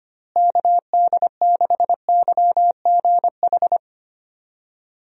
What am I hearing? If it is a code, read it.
Morse code, heard as a keyed tone, about 25 wpm, 700 Hz. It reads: KD6YGH
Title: USA Call Signs - 6 Characters - Repeated 2x - 25wpm